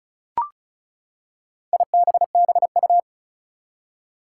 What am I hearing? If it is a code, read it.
Morse code, heard as a keyed tone, about 36 wpm, 700 Hz. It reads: IBBU
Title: Four Challenging Letter-Number Combinations 35wpm